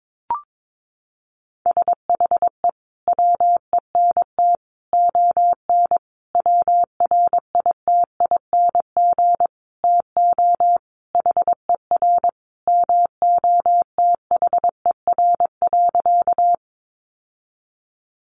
Morse code, heard as a keyed tone, about 22 wpm, 700 Hz. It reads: SHE WENT ON WRITING TO HER MOTHER.
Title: Sentences from Top 800 Words 22wpm